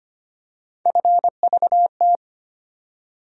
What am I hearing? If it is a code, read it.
Morse code, heard as a keyed tone, about 25 wpm, 700 Hz. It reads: FVT
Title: USA Call Sign Postfixes - 3 Letters 25wpm